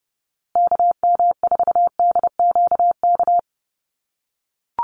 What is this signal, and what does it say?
Morse code, heard as a keyed tone, about 30 wpm, 700 Hz. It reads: KM4DQK
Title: USA Call Signs Repeated 2x 30wpm